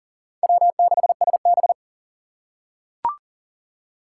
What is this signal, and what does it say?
Morse code, heard as a keyed tone, about 40 wpm, 700 Hz. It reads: W6SB
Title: USA Call Signs - 4 Characters - 6x Character Spacing - 40wpm